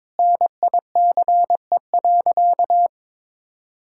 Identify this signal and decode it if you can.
Morse code, heard as a keyed tone, about 22 wpm, 700 Hz. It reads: NICE.
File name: practice / 22wpm / Sentences with 2-Word Phrases - 22wpm